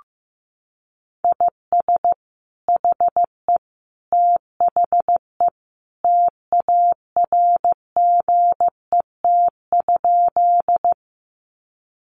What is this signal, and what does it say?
Morse code, heard as a keyed tone, about 15 wpm, 700 Hz. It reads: IS HE THE TARGET?
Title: Sentences from Top 1200 Words 15wpm